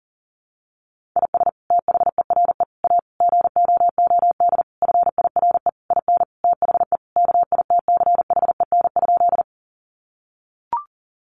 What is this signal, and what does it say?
Morse code, heard as a keyed tone, about 40 wpm, 700 Hz. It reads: IS THERE A GOOD FIRE IN THE KITCHEN?
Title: Sentences from Top 1400 Words 40wpm